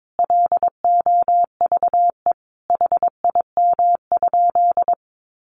Morse code, heard as a keyed tone, about 22 wpm, 700 Hz. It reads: LOVE HIM?
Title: Sentences from Top 1100 Words 22wpm